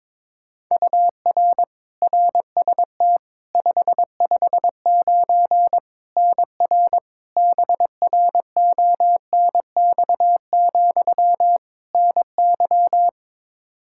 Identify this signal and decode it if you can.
Morse code, heard as a keyed tone, about 22 wpm, 700 Hz. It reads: UR RST 559 NR BRONX, NY
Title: K1USN SST QSO Practice 22wpm